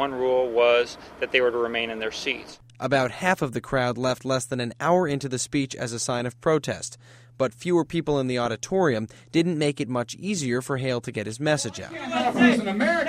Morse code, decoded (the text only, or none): none